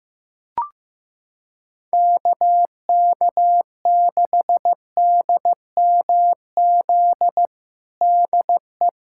Morse code, heard as a keyed tone, about 15 wpm, 700 Hz. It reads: KK6DMZ DE
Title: USA Callsign DE Callsign - 6 Characters - 15wpm